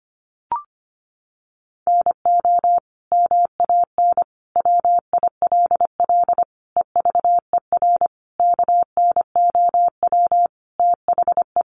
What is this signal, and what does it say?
Morse code, heard as a keyed tone, about 25 wpm, 700 Hz. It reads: NO MAN WILL EVER KNOW THE
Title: Sentences from Top 1300 Words 25wpm